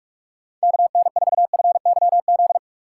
Morse code, heard as a keyed tone, about 46 wpm, 700 Hz. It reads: KN4FYZ
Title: USA Call Signs - 6 Characters - 2x Character Spacing - 45wpm